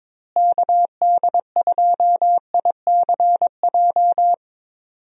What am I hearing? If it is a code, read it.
Morse code, heard as a keyed tone, about 22 wpm, 700 Hz. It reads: KD2ICJ
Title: USA Call Signs - 6 Characters - Repeated 3x - 22wpm